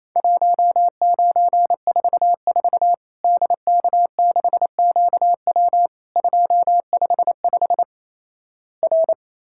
Morse code, heard as a keyed tone, about 28 wpm, 700 Hz. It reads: 1944 DK6QW 255 R
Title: Worked All Europe DX Contest - QTC Exchange - 28wpm